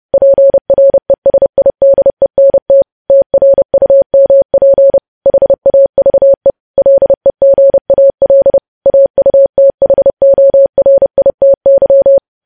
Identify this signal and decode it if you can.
Morse code, heard as a keyed tone, about 30 wpm, 550 Hz. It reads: PRESIDENT TRUMP HAVE LEGAL AUTHORITY
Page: news headlines in morse code at 30 wpm